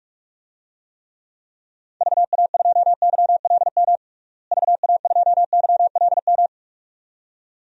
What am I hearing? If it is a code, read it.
Morse code, heard as a keyed tone, about 42 wpm, 700 Hz. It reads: VA2YLM VA2YLM
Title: DXCC-100 Call Signs - 6 Characters - Repeated 2x - 45wpm